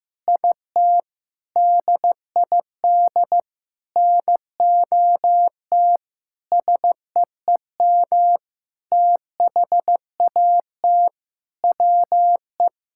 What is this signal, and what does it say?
Morse code, heard as a keyed tone, about 15 wpm, 700 Hz. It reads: IT DID NOT SEEM THAT WE
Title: Sentences from Top 1000 Words - Review Part 6 - 15wpm